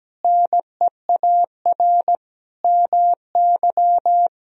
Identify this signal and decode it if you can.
Morse code, heard as a keyed tone, about 17 wpm, 700 Hz. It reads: NEAR MY
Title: Sentences from Top 1000 Words - Review Part 1 - 17wpm